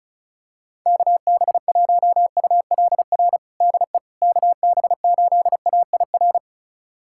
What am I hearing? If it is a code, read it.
Morse code, heard as a keyed tone, about 36 wpm, 700 Hz. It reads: KB1ULR DE KB8AIR
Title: USA Callsign DE Callsign 35wpm